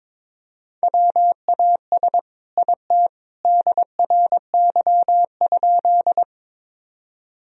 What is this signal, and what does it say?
Morse code, heard as a keyed tone, about 22 wpm, 700 Hz. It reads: WAS IT DRY?
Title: Sentences from Top 1000 Words - Review Part 3 - 22wpm